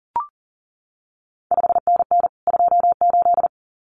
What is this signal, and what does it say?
Morse code, heard as a keyed tone, about 41 wpm, 700 Hz. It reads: 5NN 28
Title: CQ WW DX Contest Exchange Encore - 40wpm